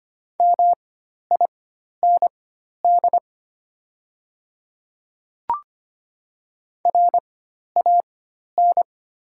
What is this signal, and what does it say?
Morse code, heard as a keyed tone, about 25 wpm, spaced as for 12 wpm, 700 Hz. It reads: MIND RAN
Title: 4x Character Spacing - Top 500 Words - 25wpm